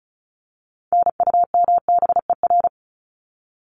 Morse code, heard as a keyed tone, about 34 wpm, 700 Hz. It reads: NUMBER
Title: ES:S Suffix - Top 100 Words with Base Words 35wpm